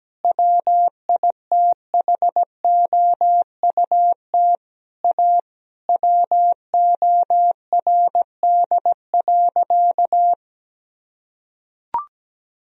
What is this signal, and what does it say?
Morse code, heard as a keyed tone, about 17 wpm, 700 Hz. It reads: WITHOUT A WORD.
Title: Sentences from Top 1000 Words - Review Part 3 - 17wpm